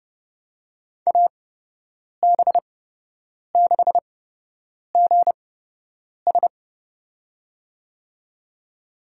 Morse code, heard as a keyed tone, about 30 wpm, spaced as for 8 wpm, 700 Hz. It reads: AB6GS
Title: USA Call Signs - 5 Characters - 8x Character Spacing - 30wpm